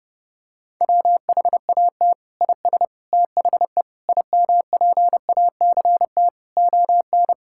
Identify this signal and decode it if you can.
Morse code, heard as a keyed tone, about 32 wpm, 700 Hz. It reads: WHAT IS THE IMPACT ON
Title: Sentences from Top 1000 Words - Review Part 6 - 30wpm